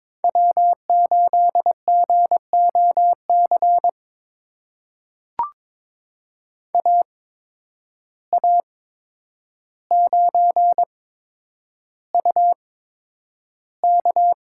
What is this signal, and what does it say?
Morse code, heard as a keyed tone, about 22 wpm, spaced as for 9 wpm, 700 Hz. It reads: W8GOC AA9UK
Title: USA Call Signs - 5 Characters - 8x Character Spacing - 22wpm